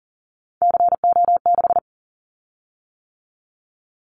Morse code, heard as a keyed tone, about 38 wpm, 700 Hz. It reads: CO6
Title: Callsign Prefixes for Top 100 DXCC - 40wpm